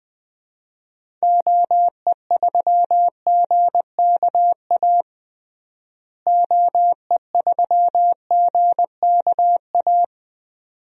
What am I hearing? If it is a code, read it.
Morse code, heard as a keyed tone, about 20 wpm, 700 Hz. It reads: OE3GKA OE3GKA
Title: DXCC-100 Call Signs - 6 Characters - Repeated 2x - 20wpm